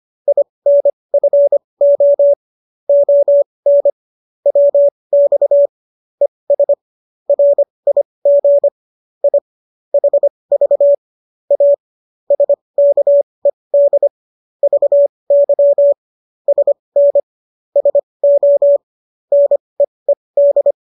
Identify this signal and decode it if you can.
Morse code, heard as a keyed tone, about 25 wpm, spaced as for 20 wpm, 570 Hz. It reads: INFO ON WX ES RIG I HV A SKED VY SN SO NEED
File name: qso221 20